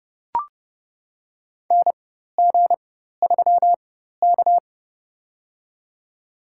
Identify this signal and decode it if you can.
Morse code, heard as a keyed tone, about 30 wpm, spaced as for 14 wpm, 700 Hz. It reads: NG3K
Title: USA Call Signs - 4x Character Spacing - 30wpm